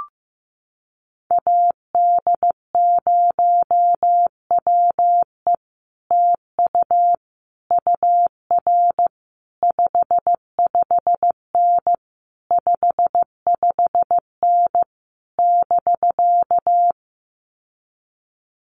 Morse code, heard as a keyed tone, about 15 wpm, 700 Hz. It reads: AD0WE TU UR 55N 55N <BK>
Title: Common QSO Phrases Encore 15wpm